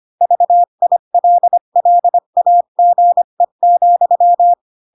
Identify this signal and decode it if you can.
Morse code, heard as a keyed tone, about 25 wpm, 700 Hz. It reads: VILLAGE,